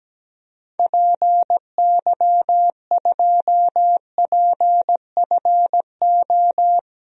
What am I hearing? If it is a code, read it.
Morse code, heard as a keyed tone, about 17 wpm, 700 Hz. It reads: PY2PFO
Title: DXCC-100 Call Signs - 6 Characters - Repeated 2x - 17wpm